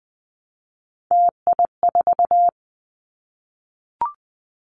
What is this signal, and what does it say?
Morse code, heard as a keyed tone, about 20 wpm, 700 Hz. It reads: TI4
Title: Callsign Prefixes for Top 100 DXCC - 20wpm